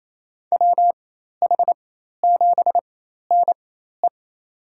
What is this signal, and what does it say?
Morse code, heard as a keyed tone, about 28 wpm, spaced as for 13 wpm, 700 Hz. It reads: WH7NE
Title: USA Call Signs - 4x Character Spacing - 28wpm